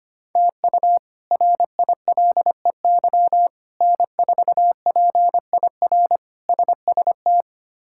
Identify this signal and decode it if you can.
Morse code, heard as a keyed tone, about 25 wpm, 700 Hz. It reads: TU RILEY N4PIR SST
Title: K1USN SST QSO Practice - 25wpm